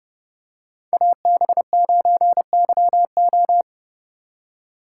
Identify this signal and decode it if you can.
Morse code, heard as a keyed tone, about 29 wpm, 700 Hz. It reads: AB9YO